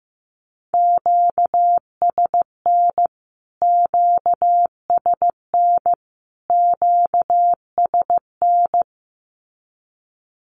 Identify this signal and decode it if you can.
Morse code, heard as a keyed tone, about 15 wpm, 700 Hz. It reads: QSN QSN QSN
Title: Q-Codes Repeated 3x 15wpm